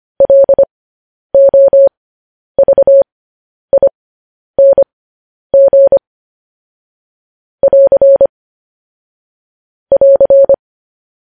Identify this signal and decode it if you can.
Morse code, heard as a keyed tone, about 25 wpm, spaced as for 10 wpm, 550 Hz. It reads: LOVING <AR> <AR>
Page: news-headlines-in-morse-code-at-10-wpm